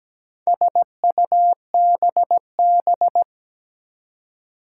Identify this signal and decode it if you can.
Morse code, heard as a keyed tone, about 17 wpm, 700 Hz. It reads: SUBB
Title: Four Challenging Letter-Number Combinations 17wpm